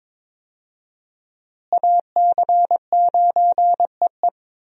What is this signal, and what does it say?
Morse code, heard as a keyed tone, about 22 wpm, 700 Hz. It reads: AC9EE
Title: USA Call Signs - 5 Characters - Repeated 3x - 22wpm